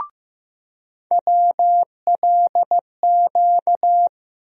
Morse code, heard as a keyed tone, about 15 wpm, 700 Hz. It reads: WLQ
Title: USA Call Sign Postfixes - 3 Letters 15wpm